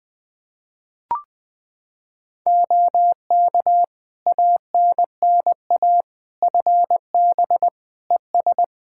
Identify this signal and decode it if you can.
Morse code, heard as a keyed tone, about 20 wpm, 700 Hz. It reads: OK ANNA FB ES
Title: Practice QSO - Exchange 2 - Info - 20wpm